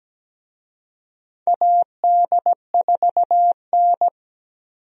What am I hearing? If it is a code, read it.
Morse code, heard as a keyed tone, about 17 wpm, 700 Hz. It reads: AD4N
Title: USA Call Signs - 4 Characters - 17wpm